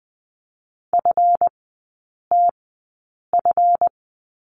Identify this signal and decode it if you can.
Morse code, heard as a keyed tone, about 20 wpm, spaced as for 8 wpm, 690 Hz. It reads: FTF
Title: 145 Lesson 24e - Letter y - Three Characters 20wpm